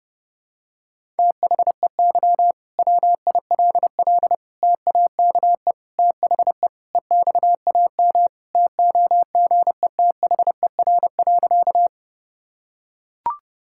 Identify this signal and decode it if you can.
Morse code, heard as a keyed tone, about 28 wpm, 700 Hz. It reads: THEY WILL TAKE THE EXAM TOGETHER.
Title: Sentences from Top 1000 Words - Review Part 5 - 30wpm